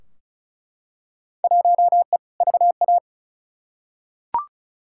Morse code, heard as a keyed tone, about 34 wpm, 700 Hz. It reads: 1E VA
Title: Field Day Exchanges - 35wpm